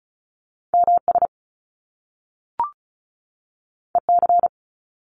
Morse code, heard as a keyed tone, about 36 wpm, 700 Hz. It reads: MS EC
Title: Top Two Letter Combinations in Words 35wpm